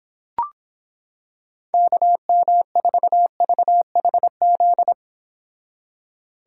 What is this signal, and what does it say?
Morse code, heard as a keyed tone, about 26 wpm, 700 Hz. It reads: KM4VHZ